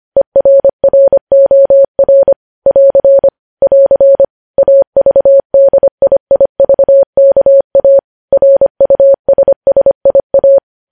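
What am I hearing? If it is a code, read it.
Morse code, heard as a keyed tone, about 25 wpm, 550 Hz. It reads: ERROR <AR> <AR> AVDIIVKA RUSSIA
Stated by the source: News headlines sent in Morse at 25.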